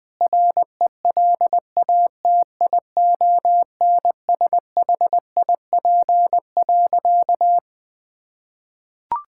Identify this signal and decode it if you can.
Morse code, heard as a keyed tone, about 20 wpm, 700 Hz. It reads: RELATIONSHIP.
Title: Sentences with 5-Word Phrases - Spoken Beginning - 20wpm